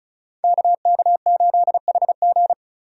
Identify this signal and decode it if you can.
Morse code, heard as a keyed tone, about 33 wpm, 700 Hz. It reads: KK8HG
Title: Code Groups - 5 Letter:Numbers - 35wpm